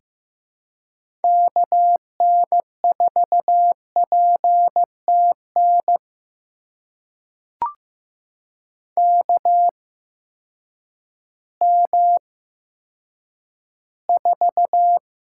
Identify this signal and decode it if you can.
Morse code, heard as a keyed tone, about 15 wpm, spaced as for 9 wpm, 700 Hz. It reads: KN4PTN KM4
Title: USA Call Signs - 8x Character Spacing - 15wpm